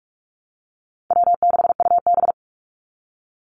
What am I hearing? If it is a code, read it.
Morse code, heard as a keyed tone, about 44 wpm, 700 Hz. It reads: W6UB
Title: NA SOTA Chaser Callsigns 45wpm